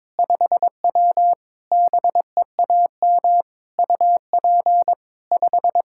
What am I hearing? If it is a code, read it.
Morse code, heard as a keyed tone, about 21 wpm, 700 Hz. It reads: 5W BEAM UP 5
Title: Practice QSO - Exchange 2 - Info - 22wpm